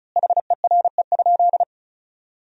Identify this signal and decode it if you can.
Morse code, heard as a keyed tone, about 36 wpm, 700 Hz. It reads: HERE?
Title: 4x Word-Spacing - Sentences from Top 200 Words - 35wpm